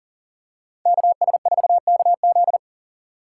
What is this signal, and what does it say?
Morse code, heard as a keyed tone, about 40 wpm, 700 Hz. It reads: KS4KZ
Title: USA Call Signs - 5 Characters - 3x Character Spacing - 40wpm